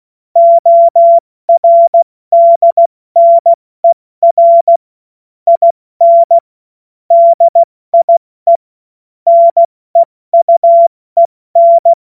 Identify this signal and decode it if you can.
Morse code, heard as a keyed tone, about 16 wpm, spaced as for 14 wpm, 680 Hz. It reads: ORDNER IN DIE NEUEN
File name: Witz-07-Oktober-2025-70-F80